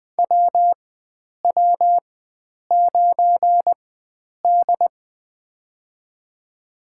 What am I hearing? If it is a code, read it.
Morse code, heard as a keyed tone, about 21 wpm, spaced as for 9 wpm, 700 Hz. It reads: WW9D
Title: USA Call Signs - 4x Character Spacing - 20wpm